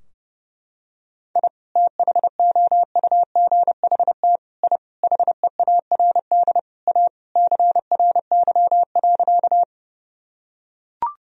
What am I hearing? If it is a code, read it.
Morse code, heard as a keyed tone, about 32 wpm, 700 Hz. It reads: I THOUGHT I HEARD A CRY.
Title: Sentences from Top 1000 Words - Review Part 2 - 30wpm